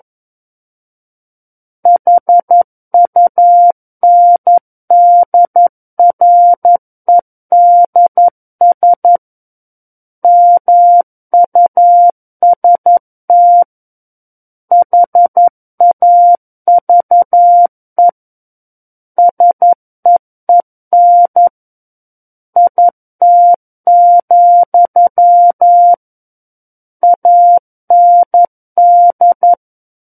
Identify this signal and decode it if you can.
Morse code, heard as a keyed tone, about 11 wpm, 700 Hz. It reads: HUNDREDS MUST HAVE SEEN IT, AND
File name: 11 WPM